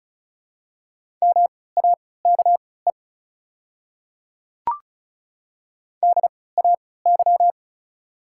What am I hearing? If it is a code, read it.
Morse code, heard as a keyed tone, about 36 wpm, spaced as for 20 wpm, 700 Hz. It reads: MAKE DAY
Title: 3x Character Spacing - Top 100 Words - 35wpm